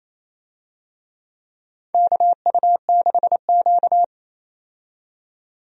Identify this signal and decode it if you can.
Morse code, heard as a keyed tone, about 28 wpm, 700 Hz. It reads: KU6Q
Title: USA Call Signs - 4 Characters - 28wpm